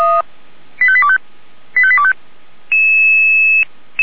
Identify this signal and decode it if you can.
Five-tone selective call (ZVEI), heard as quick runs of five tones, tones of about 70 ms. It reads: 85825 85825
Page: ZVEI